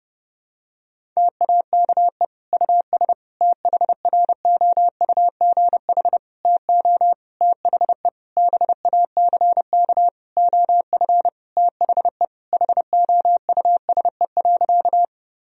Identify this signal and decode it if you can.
Morse code, heard as a keyed tone, about 30 wpm, 700 Hz. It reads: TAKE US THROUGH TO THE BACK OF THE HOUSE.
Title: Sentences from Top 1000 Words - Review Part 1 - 30wpm